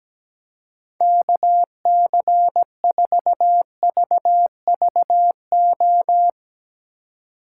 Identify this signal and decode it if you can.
Morse code, heard as a keyed tone, about 19 wpm, 700 Hz. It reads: KC4VVO
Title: Speed Racing - USA 2x3 Call Signs 17wpm